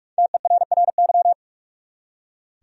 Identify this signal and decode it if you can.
Morse code, heard as a keyed tone, about 46 wpm, 700 Hz. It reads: TERRY
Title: USA Common Names - Max 5 Letters - 45wpm